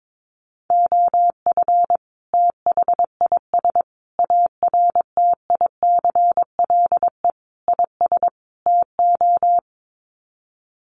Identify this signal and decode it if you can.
Morse code, heard as a keyed tone, about 22 wpm, 700 Hz. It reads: OF THIS ARTICLE IS TO
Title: Sets of 5 Words Encore2 22wpm